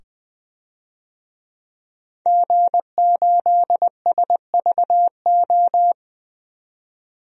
Morse code, heard as a keyed tone, about 20 wpm, 700 Hz. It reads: G8SVO
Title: DXCC-100 Call Signs - 5 Characters - 20wpm